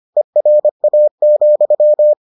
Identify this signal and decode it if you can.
Morse code, heard as a keyed tone, about 25 wpm, 600 Hz. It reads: ERA,